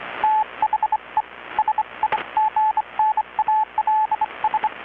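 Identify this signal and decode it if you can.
Morse code, heard as a keyed tone, about 25 wpm, 860 Hz, with radio noise. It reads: THE SIGNALS
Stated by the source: Morse from KPH's Night of Nights XX maritime radiotelegraphy event, received in Chiapas, Mexico about 3,800 km away on a Kenwood R-600 with a grounded 10 m long-wire antenna.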